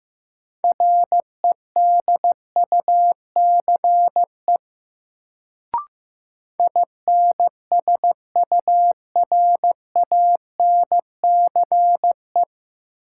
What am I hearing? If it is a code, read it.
Morse code, heard as a keyed tone, about 15 wpm, 700 Hz. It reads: REDUCE INSURANCE
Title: Sentences from Top 1100 Words - New Word Drill Alt - 15wpm